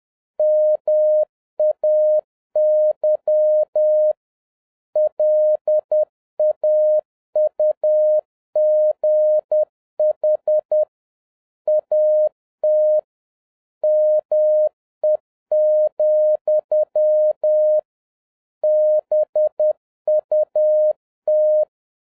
Morse code, heard as a keyed tone, about 10 wpm, 610 Hz. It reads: MAY LAUGH AT ME, BUT